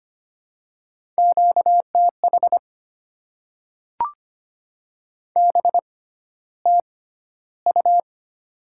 Morse code, heard as a keyed tone, about 25 wpm, spaced as for 13 wpm, 700 Hz. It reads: QTH BTU
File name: Top 20 QSO Elements - 6x Character Spacing - 25wpm